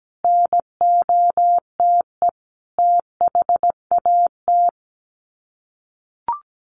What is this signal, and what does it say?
Morse code, heard as a keyed tone, about 17 wpm, 700 Hz. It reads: NOTE THAT
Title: Sets of 5 Words Encore2 17wpm